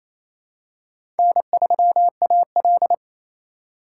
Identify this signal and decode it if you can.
Morse code, heard as a keyed tone, about 27 wpm, 700 Hz. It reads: N3AL
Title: USA Call Signs - 4x Character Spacing - 28wpm